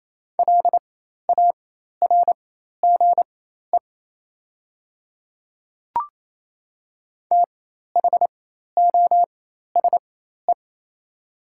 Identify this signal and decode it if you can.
Morse code, heard as a keyed tone, about 28 wpm, spaced as for 13 wpm, 700 Hz. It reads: LARGE THOSE
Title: 4x Character Spacing - Top 300 Words - 28wpm